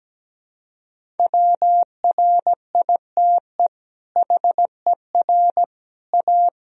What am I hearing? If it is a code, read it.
Morse code, heard as a keyed tone, about 17 wpm, 700 Hz. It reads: WRITE HER A
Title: Sentences from Top 1000 Words - Review Part 1 - 17wpm